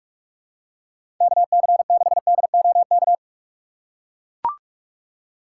Morse code, heard as a keyed tone, about 41 wpm, 700 Hz. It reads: KC6DOK
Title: USA Call Signs - 6 Characters - 3x Character Spacing - 45wpm